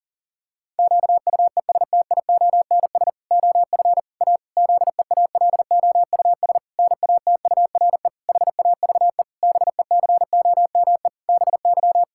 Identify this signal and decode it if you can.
Morse code, heard as a keyed tone, about 40 wpm, 700 Hz. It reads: QUESTIONS OF A ZEALOUS NATURE HAVE BECOME BY
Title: Pangram Challenge v2 - 40wpm